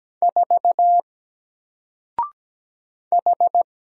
Morse code, heard as a keyed tone, about 17 wpm, 700 Hz. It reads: 4 H
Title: Single Character (Letter:Number:Punctuation:Prosign) Alt 17wpm